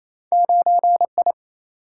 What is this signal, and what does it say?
Morse code, heard as a keyed tone, about 28 wpm, 700 Hz. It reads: 9I
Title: Single-Letter Single-Number 28wpm